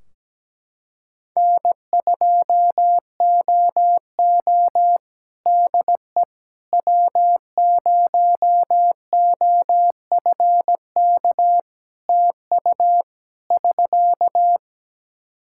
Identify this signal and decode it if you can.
Morse code, heard as a keyed tone, about 17 wpm, 700 Hz. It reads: N2OO DE W0OFK TU <SK>
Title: K1USN SST QSO Practice 17wpm